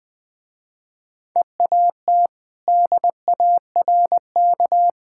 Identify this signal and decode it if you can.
Morse code, heard as a keyed tone, about 21 wpm, 700 Hz. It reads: EAT DARK
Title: Easy Sentences with Hints - 20wpm